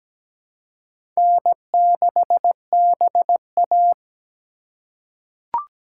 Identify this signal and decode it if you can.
Morse code, heard as a keyed tone, about 17 wpm, 700 Hz. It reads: N6BA
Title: USA Call Signs - 4 Characters - 4x Character Spacing - 17wpm